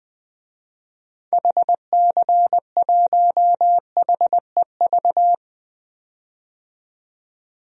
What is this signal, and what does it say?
Morse code, heard as a keyed tone, about 21 wpm, 700 Hz. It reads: HC1HEV
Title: DXCC-100 Call Signs - 6 Characters - 20wpm